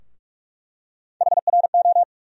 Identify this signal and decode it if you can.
Morse code, heard as a keyed tone, about 42 wpm, 700 Hz. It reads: HRO
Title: USA Call Sign Postfixes - 3 Letters 45wpm